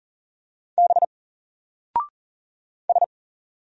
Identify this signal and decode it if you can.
Morse code, heard as a keyed tone, about 40 wpm, 700 Hz. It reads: B S
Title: Single Character (Letter:Number:Punctuation:Prosign) Alt 40wpm